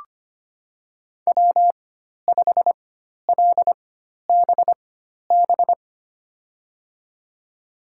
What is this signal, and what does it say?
Morse code, heard as a keyed tone, about 25 wpm, spaced as for 12 wpm, 700 Hz. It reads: W5LBB
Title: USA Call Signs - 4x Character Spacing - 25wpm